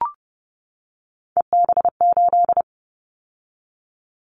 Morse code, heard as a keyed tone, about 30 wpm, 700 Hz. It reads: EB8
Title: Callsign Prefixes for Top 50 DXCC - 30wpm